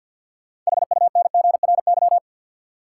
Morse code, heard as a keyed tone, about 53 wpm, 700 Hz. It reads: HUNGRY
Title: Sentences from Top 1400 Words - New Word Drill - 50wpm